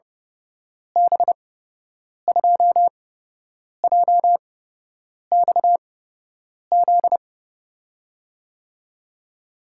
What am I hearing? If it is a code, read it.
Morse code, heard as a keyed tone, about 30 wpm, spaced as for 8 wpm, 700 Hz. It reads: B2JXZ
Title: USA Call Signs - 6 Characters - 8x Character Spacing - 30wpm